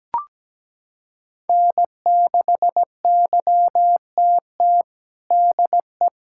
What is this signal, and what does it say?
Morse code, heard as a keyed tone, about 17 wpm, 700 Hz. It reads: N6YTT DE
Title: USA Callsign DE Callsign 17wpm